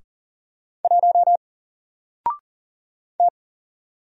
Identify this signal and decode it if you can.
Morse code, heard as a keyed tone, about 40 wpm, 700 Hz. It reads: 1 T
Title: Single Letter-Number 40wpm